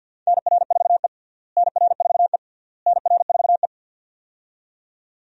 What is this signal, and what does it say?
Morse code, heard as a keyed tone, about 52 wpm, 700 Hz. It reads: NR4E NR4E NR4E
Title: USA Call Signs - 4 Characters - Repeated 3x - 50wpm